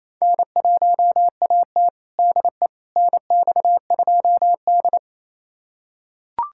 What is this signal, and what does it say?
Morse code, heard as a keyed tone, about 27 wpm, 700 Hz. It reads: N1AT DE NX2D
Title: USA Callsign DE Callsign - 4 Characters - 28wpm